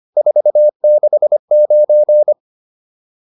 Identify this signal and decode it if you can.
Morse code, heard as a keyed tone, about 25 wpm, 600 Hz. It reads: 469